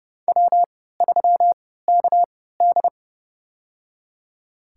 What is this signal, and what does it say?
Morse code, heard as a keyed tone, about 29 wpm, spaced as for 17 wpm, 700 Hz. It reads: W3KD
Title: USA Call Signs - 4 Characters - 3x Character Spacing - 30wpm